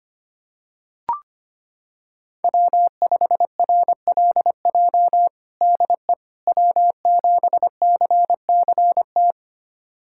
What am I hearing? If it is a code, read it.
Morse code, heard as a keyed tone, about 25 wpm, 700 Hz. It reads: W5RLJ DE W7CCT
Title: USA Callsign DE Callsign - 5 Characters - 25wpm